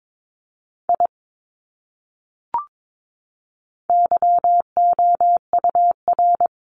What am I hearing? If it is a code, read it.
Morse code, heard as a keyed tone, about 22 wpm, 700 Hz. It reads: I YOUR
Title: Top 200 Words Repeated 22wpm